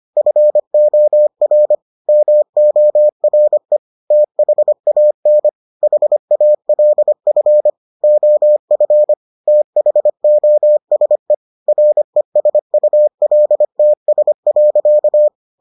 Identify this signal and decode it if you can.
Morse code, heard as a keyed tone, about 25 wpm, 600 Hz. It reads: FOR MORE THAN HALF OF THOSE RESULTS.